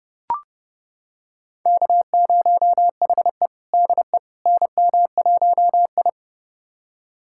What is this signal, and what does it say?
Morse code, heard as a keyed tone, about 30 wpm, 700 Hz. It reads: K0HE DE NM1I
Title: USA Callsign DE Callsign - 4 Characters - 30wpm